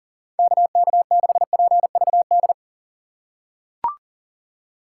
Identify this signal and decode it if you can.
Morse code, heard as a keyed tone, about 40 wpm, 700 Hz. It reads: KK6PVD
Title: USA Call Signs - 6 Characters - 4x Character Spacing - 40wpm